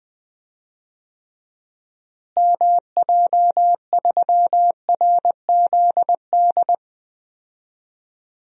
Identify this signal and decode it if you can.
Morse code, heard as a keyed tone, about 21 wpm, 700 Hz. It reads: MJ3RZD
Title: DXCC-100 Call Signs - 6 Characters - Repeated 3x - 20wpm